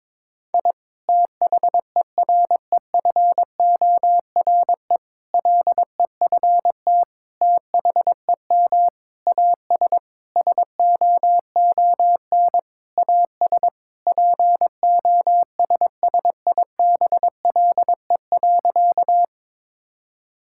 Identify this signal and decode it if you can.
Morse code, heard as a keyed tone, about 22 wpm, 700 Hz. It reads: I THEREFORE LEFT THEM AS SOON AS POSSIBLE.
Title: Sentences from Top 1000 Words - Review Part 5 - 22wpm